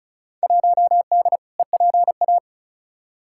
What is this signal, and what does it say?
Morse code, heard as a keyed tone, about 34 wpm, 700 Hz. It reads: 1D EPA
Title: Field Day Exchanges - 35wpm